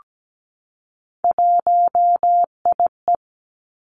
Morse code, heard as a keyed tone, about 17 wpm, 700 Hz. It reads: 1IE
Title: Three Challenging Letter-Number Combinations 17wpm